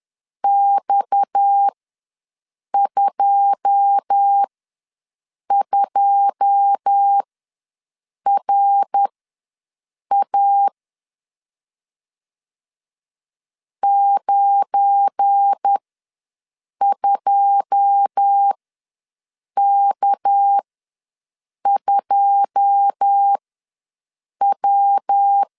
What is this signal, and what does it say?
Morse code, heard as a keyed tone, about 10 wpm, spaced as for 6 wpm, 800 Hz. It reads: X22RA 92K2W
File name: LEZIONI / 9 2 ws 10wpm 4130